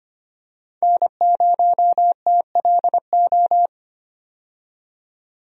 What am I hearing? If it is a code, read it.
Morse code, heard as a keyed tone, about 25 wpm, 700 Hz. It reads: N0TLO
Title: USA Call Signs - 5 Characters - 25wpm